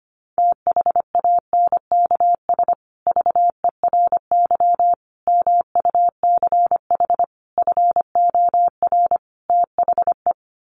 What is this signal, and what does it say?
Morse code, heard as a keyed tone, about 25 wpm, 700 Hz. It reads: THANKS VERY MUCH FOR THE
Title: Sets of 5 Words Encore2 25wpm